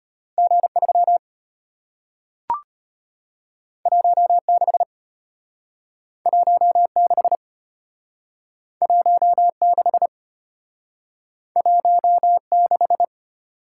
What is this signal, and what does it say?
Morse code, heard as a keyed tone, about 35 wpm, 700 Hz. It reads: G3 16 16 16 16